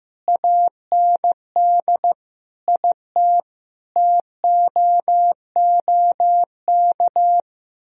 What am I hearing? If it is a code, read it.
Morse code, heard as a keyed tone, about 15 wpm, 700 Hz. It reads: AND IT TOOK
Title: Sentences from Top 1000 Words - Review Part 4 - 15wpm